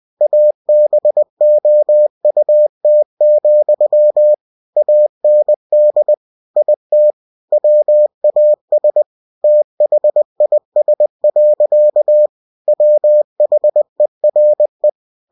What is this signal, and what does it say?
Morse code, heard as a keyed tone, about 20 wpm, 600 Hz. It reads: ABOUT, AND IT WAS THIS. WHERE
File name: cw-books / winnie-the-pooh / winnie-20wpm-page-0035